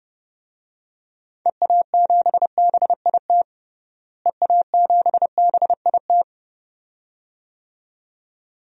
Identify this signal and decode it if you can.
Morse code, heard as a keyed tone, about 30 wpm, 700 Hz. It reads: EA7BIT EA7BIT
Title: DXCC-100 Call Signs - 6 Characters - Repeated 2x - 30wpm